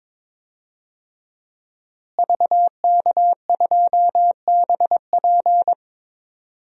Morse code, heard as a keyed tone, about 21 wpm, 700 Hz. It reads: VK2BP
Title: DXCC-100 Call Signs - 5 Characters - Repeated 2x - 22wpm